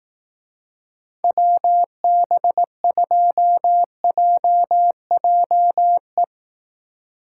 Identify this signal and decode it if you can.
Morse code, heard as a keyed tone, about 18 wpm, 700 Hz. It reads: WB2JJE
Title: Speed Racing - USA 2x3 Call Signs 15wpm